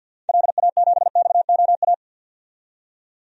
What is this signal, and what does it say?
Morse code, heard as a keyed tone, about 47 wpm, 700 Hz. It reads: LA7XOA
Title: DXCC-100 Call Signs - 6 Characters - 50wpm